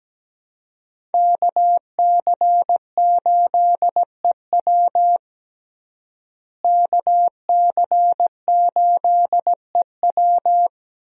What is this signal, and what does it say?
Morse code, heard as a keyed tone, about 17 wpm, 700 Hz. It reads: KC8EW KC8EW
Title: USA Call Signs - 5 Characters - Repeated 3x - 17wpm